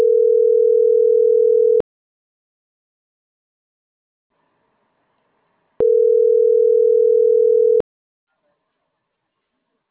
Ringback tone (North America), 440 + 480 Hz.